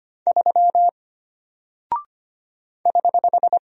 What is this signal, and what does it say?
Morse code, heard as a keyed tone, about 25 wpm, 700 Hz. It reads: 3 <HH>
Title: Single Character (Letter:Number:Punctuation:Prosign) Alt 25wpm